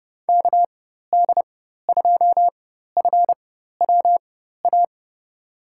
Morse code, heard as a keyed tone, about 32 wpm, spaced as for 14 wpm, 700 Hz. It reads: KD2FWA
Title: USA Call Signs - 6 Characters - 4x Character Spacing - 30wpm